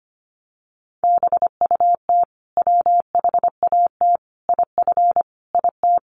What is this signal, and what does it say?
Morse code, heard as a keyed tone, about 25 wpm, 700 Hz. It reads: BUT WHAT IF IT
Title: Sentences from Top 100 Words 25wpm